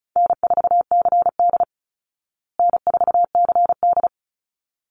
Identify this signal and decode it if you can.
Morse code, heard as a keyed tone, about 34 wpm, 700 Hz. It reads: N4CD N4CD
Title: Easy Call Signs Repeated 2x - 35wpm